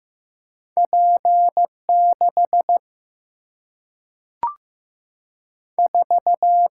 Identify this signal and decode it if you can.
Morse code, heard as a keyed tone, about 15 wpm, 700 Hz. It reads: P 6 4
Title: Single-Letter Single-Number 15wpm